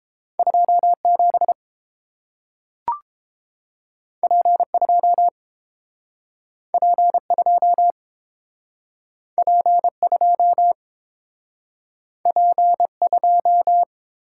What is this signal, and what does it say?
Morse code, heard as a keyed tone, about 30 wpm, 700 Hz. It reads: 27 P2 P2 P2 P2